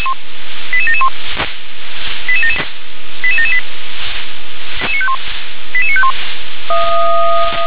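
Five-tone selective call (ZVEI), heard as quick runs of five tones, tones of about 70 ms. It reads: 88771 88778 88951
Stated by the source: ZVEI five-tone call received by SDR.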